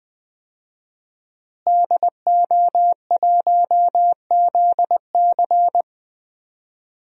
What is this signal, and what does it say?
Morse code, heard as a keyed tone, about 21 wpm, 700 Hz. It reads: DO1ZC
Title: DXCC-100 Call Signs - 5 Characters - Repeated 3x - 20wpm